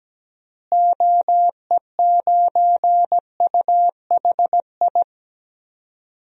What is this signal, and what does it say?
Morse code, heard as a keyed tone, about 17 wpm, 700 Hz. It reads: OE9UHI
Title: DXCC-100 Call Signs - 6 Characters - Repeated 2x - 17wpm